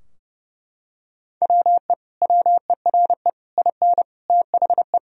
Morse code, heard as a keyed tone, about 32 wpm, 700 Hz. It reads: WE WERE IN THE
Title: Sentences from Top 1000 Words - Review Part 2 - 30wpm